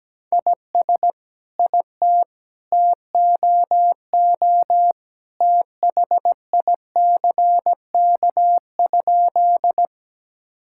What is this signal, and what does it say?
Morse code, heard as a keyed tone, about 17 wpm, 700 Hz. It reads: IS IT TOO THICK?
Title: Sentences from Top 1600 Words 17wpm